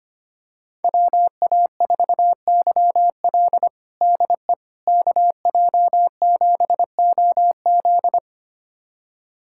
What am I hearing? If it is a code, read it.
Morse code, heard as a keyed tone, about 25 wpm, 700 Hz. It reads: WA4YL DE KJ7OZ
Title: USA Callsign DE Callsign 25wpm